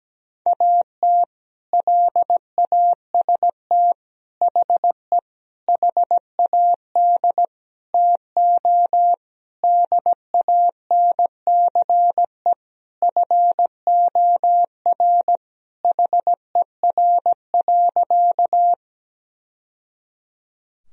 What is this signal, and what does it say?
Morse code, heard as a keyed tone, about 17 wpm, 700 Hz. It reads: AT LAST HE HAD TO DANCE FOR HER.
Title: Sentences from Top 1000 Words - Review Part 4 - 17wpm